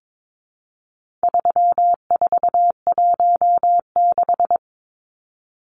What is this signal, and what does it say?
Morse code, heard as a keyed tone, about 22 wpm, 700 Hz. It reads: 3416
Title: Numbers - 4-Digit 22wpm